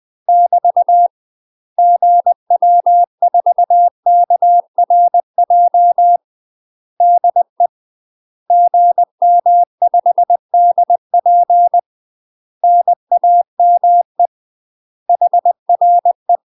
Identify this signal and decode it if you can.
Morse code, heard as a keyed tone, about 20 wpm, 700 Hz. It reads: <BT> GW4KRJ DE GM5DP NAME HRE
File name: UK MASTER QSOs TRAINING / 20WPM / UK MASTER QSOs 0001